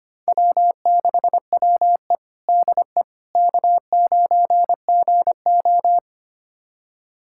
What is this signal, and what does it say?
Morse code, heard as a keyed tone, about 25 wpm, 700 Hz. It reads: W6WE DE K9GO
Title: USA Callsign DE Callsign - 4 Characters - 25wpm